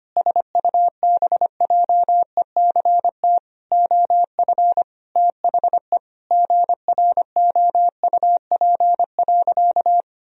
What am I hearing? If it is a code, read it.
Morse code, heard as a keyed tone, about 25 wpm, 700 Hz. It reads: SUBJECT OF THE GROUP.
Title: Sentences from Top 1000 Words - Review Part 1 - 25wpm